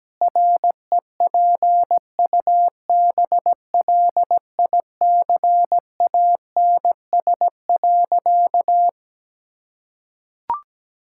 Morse code, heard as a keyed tone, about 17 wpm, 700 Hz. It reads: REPUBLICANS.
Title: Pangram Challenge v2 - 17wpm